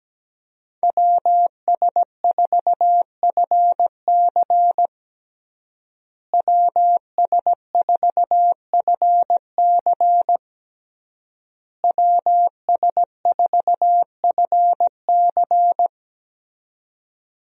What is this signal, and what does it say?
Morse code, heard as a keyed tone, about 17 wpm, 700 Hz. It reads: WS4FC WS4FC WS4FC
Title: USA Call Signs - 5 Characters - Repeated 3x - 17wpm